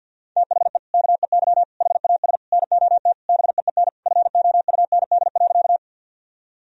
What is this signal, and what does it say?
Morse code, heard as a keyed tone, about 50 wpm, 700 Hz. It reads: THE KEY HAS NOT BEEN FOUND.
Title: Sentences from Top 1000 Words - Review Part 4 - 50wpm